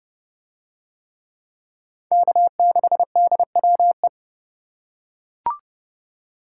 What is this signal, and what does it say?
Morse code, heard as a keyed tone, about 27 wpm, 700 Hz. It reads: K6DWE